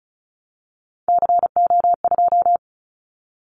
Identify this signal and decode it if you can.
Morse code, heard as a keyed tone, about 36 wpm, 700 Hz. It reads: CO2
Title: Callsign Prefixes for Top 100 DXCC - 35wpm